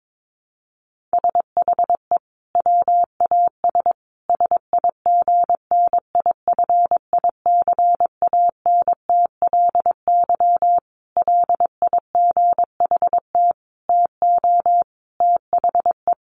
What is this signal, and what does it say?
Morse code, heard as a keyed tone, about 22 wpm, 700 Hz. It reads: SHE WAS SIGNIFICANTLY LIGHT TO THE
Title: Sentences from Top 1500 Words 22wpm